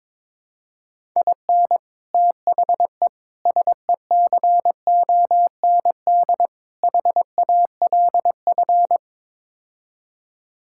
Morse code, heard as a keyed tone, about 22 wpm, 700 Hz. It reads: IN THE SECOND HALF
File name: practice / 22wpm / Build A Phrase - 5-Word Phrases - 22wpm